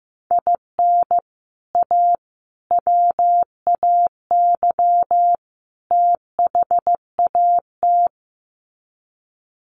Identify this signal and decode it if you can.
Morse code, heard as a keyed tone, about 15 wpm, 700 Hz. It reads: IN A WAY THAT
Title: Sets of 4 Words 15wpm